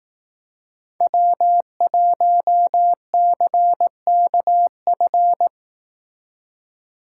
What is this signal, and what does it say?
Morse code, heard as a keyed tone, about 18 wpm, 700 Hz. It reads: W1CKF
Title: Speed Racing - USA 1x3 Call Signs 15wpm